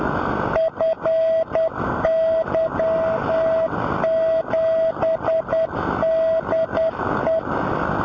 Morse code, heard as a keyed tone, about 10 wpm, 650 Hz, with noise. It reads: FY7DE
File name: FY7DE B